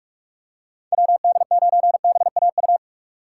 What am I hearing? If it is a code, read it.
Morse code, heard as a keyed tone, about 47 wpm, 700 Hz. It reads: WD9BAU
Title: USA Call Signs - 6 Characters - 4x Character Spacing - 45wpm